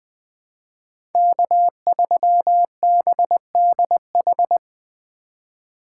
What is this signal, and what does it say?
Morse code, heard as a keyed tone, about 20 wpm, 700 Hz. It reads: K3BDH
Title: USA Call Signs - 5 Characters - 3x Character Spacing - 20wpm